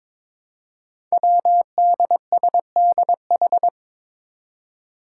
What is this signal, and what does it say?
Morse code, heard as a keyed tone, about 22 wpm, 700 Hz. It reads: WDSDH
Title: Five Challenging Letter-Number Combinations 22wpm